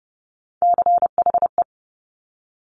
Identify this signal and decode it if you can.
Morse code, heard as a keyed tone, about 30 wpm, 690 Hz. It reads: CHE
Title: Top Three Letter Combinations in Words 30wpm